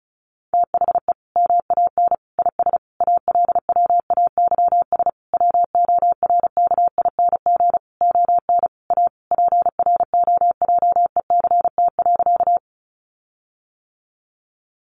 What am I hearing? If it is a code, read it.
Morse code, heard as a keyed tone, about 34 wpm, 700 Hz. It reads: THE MAN IS ALWAYS WORKING ON A PROJECT.
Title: Sentences from Top 1400 Words 35wpm